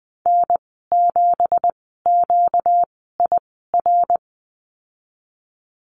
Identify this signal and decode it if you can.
Morse code, heard as a keyed tone, about 20 wpm, spaced as for 14 wpm, 700 Hz. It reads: N7QIR
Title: 238 Lesson 38g - Letter q - Call Signs 20wpm